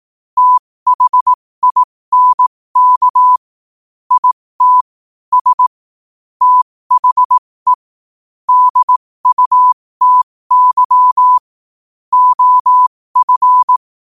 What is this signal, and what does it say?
Morse code, heard as a keyed tone, about 18 wpm, spaced as for 15 wpm, 1000 Hz. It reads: THINK IT S THE DUTY OF
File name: qotdfiles / 15wpm